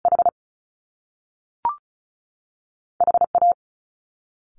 Morse code, heard as a keyed tone, about 34 wpm, 700 Hz. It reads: H H A
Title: Top Two Letter Combinations in Words 35wpm